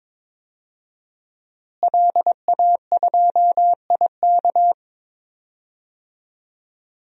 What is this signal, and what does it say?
Morse code, heard as a keyed tone, about 21 wpm, 700 Hz. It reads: LA2IK
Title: DXCC-100 Call Signs - 5 Characters - Repeated 2x - 22wpm